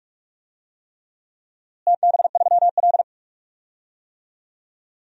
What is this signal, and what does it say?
Morse code, heard as a keyed tone, about 46 wpm, 700 Hz. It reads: TB3L
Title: DXCC-100 Call Signs - 4 Characters - Repeated 3x - 45wpm